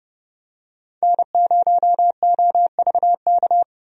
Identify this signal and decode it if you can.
Morse code, heard as a keyed tone, about 31 wpm, 700 Hz. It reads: N0OVK
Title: USA Call Signs - 5 Characters - 30wpm